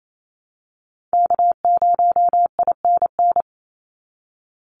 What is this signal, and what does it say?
Morse code, heard as a keyed tone, about 28 wpm, 700 Hz. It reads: K0INN
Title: Call Signs - 28wpm